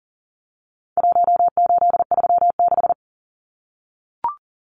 Morse code, heard as a keyed tone, about 40 wpm, 700 Hz. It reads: 1836
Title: Numbers - Contest Style - 4-Digit 40wpm